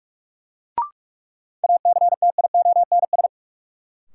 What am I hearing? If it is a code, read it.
Morse code, heard as a keyed tone, about 46 wpm, 700 Hz. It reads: ACTIONS
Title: Sentences from Top 1000 Words - Review Part 6 - New Word Drill - 45wpm